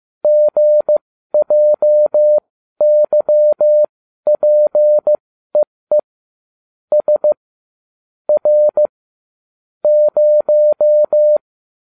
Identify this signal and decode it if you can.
Morse code, heard as a keyed tone, about 15 wpm, spaced as for 12 wpm, 600 Hz. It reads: GJYPEE S R 0